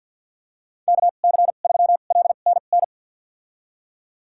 Morse code, heard as a keyed tone, about 48 wpm, spaced as for 35 wpm, 690 Hz. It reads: KC3LNN
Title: USA Call Signs - 2x Character Spacing - 50wpm